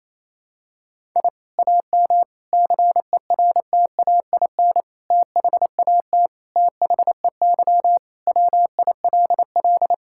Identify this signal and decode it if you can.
Morse code, heard as a keyed tone, about 27 wpm, 700 Hz. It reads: I AM CERTAIN THAT THEY WILL
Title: Sentences from Top 1000 Words - Review Part 6 - 28wpm